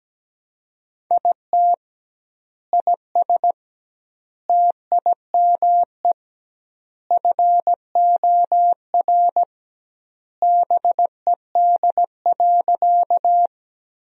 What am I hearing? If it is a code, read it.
Morse code, heard as a keyed tone, about 17 wpm, 700 Hz. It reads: IT IS TIME FOR BED.
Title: 2x Word-Spacing - Sentences from Top 500 Words - 17wpm